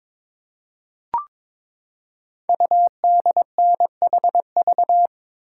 Morse code, heard as a keyed tone, about 22 wpm, 700 Hz. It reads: UDNHV
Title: Code Groups - 5 Letters - 22wpm